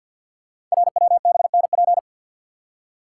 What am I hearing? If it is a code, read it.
Morse code, heard as a keyed tone, about 52 wpm, 700 Hz. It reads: RWBNP
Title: Code Groups - 5 Letters - 50wpm